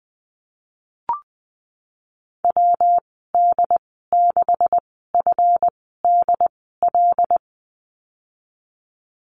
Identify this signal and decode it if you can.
Morse code, heard as a keyed tone, about 19 wpm, spaced as for 14 wpm, 700 Hz. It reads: WD6FDL
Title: 147 Lesson 24g - Letter y - Call Signs 20wpm